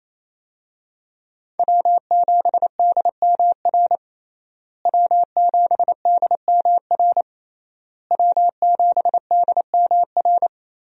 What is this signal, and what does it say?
Morse code, heard as a keyed tone, about 28 wpm, 700 Hz. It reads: W7DMR W7DMR W7DMR
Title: USA Call Signs - 5 Characters - Repeated 3x - 28wpm